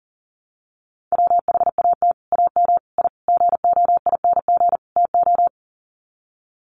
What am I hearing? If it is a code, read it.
Morse code, heard as a keyed tone, about 40 wpm, 700 Hz. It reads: WHAT AM I GOING TO
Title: Sets of 5 Words Encore 40wpm